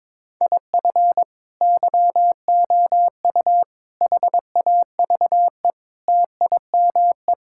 Morse code, heard as a keyed tone, about 22 wpm, 700 Hz. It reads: IF YOU HAVE TIME
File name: practice / 22wpm / Sentences with 3-Word Phrases - 22wpm